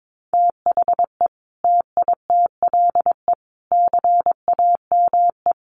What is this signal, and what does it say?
Morse code, heard as a keyed tone, about 22 wpm, 700 Hz. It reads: THE TITLE CAME
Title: Sentences from Top 1200 Words 22wpm